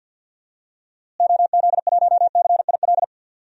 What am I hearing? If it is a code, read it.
Morse code, heard as a keyed tone, about 52 wpm, 700 Hz. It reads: OZ1CIL
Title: DXCC-100 Call Signs - 6 Characters - Repeated 2x - 50wpm